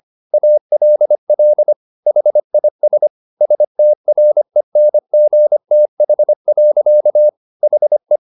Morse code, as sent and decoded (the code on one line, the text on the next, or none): .- .-.. .-.. / .... .. ... / ... - .-. -. --. - .... .-.-.- / .... .
ALL HIS STRENGTH. HE